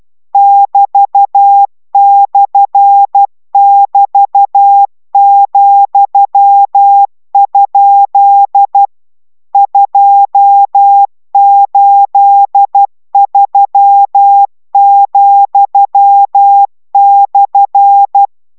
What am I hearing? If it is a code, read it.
Morse code, heard as a keyed tone, about 12 wpm, 800 Hz. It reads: <BT>/<BT>,? 283,/